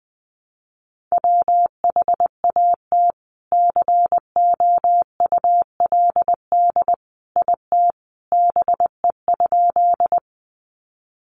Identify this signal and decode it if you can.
Morse code, heard as a keyed tone, about 20 wpm, 700 Hz. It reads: WHAT COULD IT BE?
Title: Sentences from Top 100 Words 20wpm